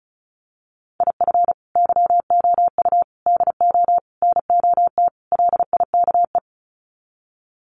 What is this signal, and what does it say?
Morse code, heard as a keyed tone, about 34 wpm, 700 Hz. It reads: IF YOU DO NOT LIKE
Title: Sets of 5 Words 35wpm